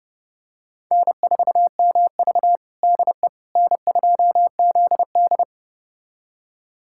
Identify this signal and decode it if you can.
Morse code, heard as a keyed tone, about 30 wpm, 700 Hz. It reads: N4MV DE N2ZD
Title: USA Callsign DE Callsign - 4 Characters - 30wpm